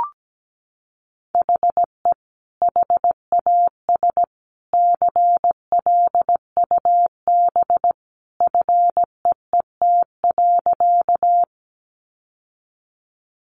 Morse code, heard as a keyed tone, about 17 wpm, 700 Hz. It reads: HE HAS CLUB FEET.
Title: Sentences from Top 1200 Words 17wpm